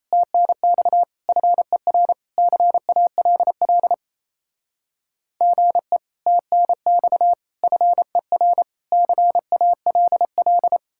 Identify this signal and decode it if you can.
Morse code, heard as a keyed tone, about 33 wpm, 700 Hz. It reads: TNX FER CALL GE TNX FER CALL
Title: Speed-Racing Common QSO Phrases - 28wpm